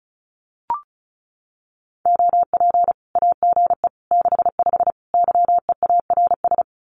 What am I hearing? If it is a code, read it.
Morse code, heard as a keyed tone, about 34 wpm, 700 Hz. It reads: OP AGE 65 YEARS
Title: Common QSO Phrases 35wpm